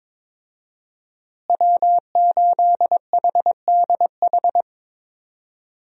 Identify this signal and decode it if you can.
Morse code, heard as a keyed tone, about 22 wpm, 700 Hz. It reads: W8HDH
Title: USA Call Signs - 5 Characters - Repeated 2x - 22wpm